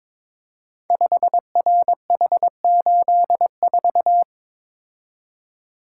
Morse code, heard as a keyed tone, about 22 wpm, 700 Hz. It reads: 5RH84
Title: Code Groups - 5 Letter:Numbers - 22wpm